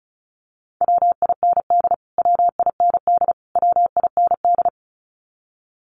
Morse code, heard as a keyed tone, about 34 wpm, 700 Hz. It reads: WIND WIND WIND
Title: Top 500 Words Repeated 35wpm